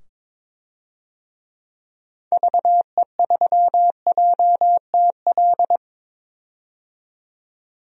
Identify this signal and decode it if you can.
Morse code, heard as a keyed tone, about 22 wpm, 700 Hz. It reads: VE3JTL
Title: DXCC-100 Call Signs - 6 Characters - 22wpm